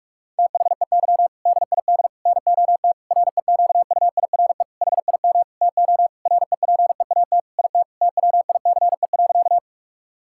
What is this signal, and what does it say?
Morse code, heard as a keyed tone, about 46 wpm, 700 Hz. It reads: THEY DID NOT REQUIRE HIM TO REPEAT IT TWICE.
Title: Sentences from Top 1600 Words 45wpm